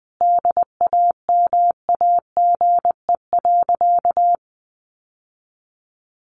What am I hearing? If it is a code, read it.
Morse code, heard as a keyed tone, about 20 wpm, 700 Hz. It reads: DAMAGE.
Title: Sentences from Top 1100 Words 20wpm